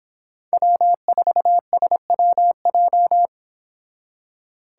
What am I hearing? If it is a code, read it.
Morse code, heard as a keyed tone, about 31 wpm, 700 Hz. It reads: W4SWJ